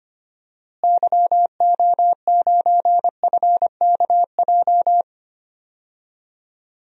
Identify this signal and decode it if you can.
Morse code, heard as a keyed tone, about 25 wpm, 700 Hz. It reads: YO9FKJ
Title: DXCC-100 Call Signs - 6 Characters - Repeated 2x - 25wpm